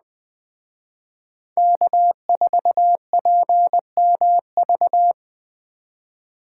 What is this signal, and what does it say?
Morse code, heard as a keyed tone, about 21 wpm, 700 Hz. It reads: K4PMV